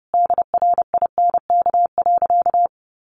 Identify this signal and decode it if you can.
Morse code, heard as a keyed tone, about 30 wpm, 700 Hz. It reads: DRINK.
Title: Sentences from Top 1500 Words 30wpm